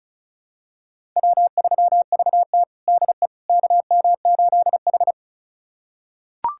Morse code, heard as a keyed tone, about 36 wpm, 700 Hz. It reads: W3VT DE KM8H
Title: USA Callsign DE Callsign - 4 Characters - 35wpm